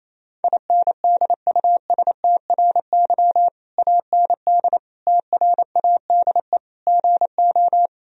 Morse code, heard as a keyed tone, about 28 wpm, 700 Hz. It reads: INDUSTRY AND TRADE GO
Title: Sentences from Top 1000 Words - Review Part 4 - 28wpm